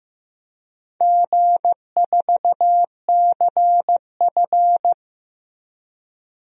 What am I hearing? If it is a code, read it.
Morse code, heard as a keyed tone, about 15 wpm, 700 Hz. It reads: G4CF
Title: DXCC-100 Call Signs - 4 Characters - Repeated 3x - 15wpm